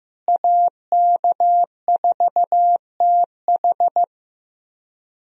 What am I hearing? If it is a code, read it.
Morse code, heard as a keyed tone, about 15 wpm, 700 Hz. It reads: AK4TH
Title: USA Call Signs - 5 Characters - Repeated 3x - 15wpm